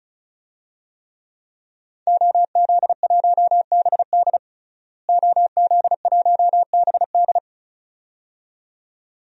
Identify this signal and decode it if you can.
Morse code, heard as a keyed tone, about 36 wpm, 700 Hz. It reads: OZ1BD OZ1BD
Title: DXCC-100 Call Signs - 5 Characters - Repeated 2x - 35wpm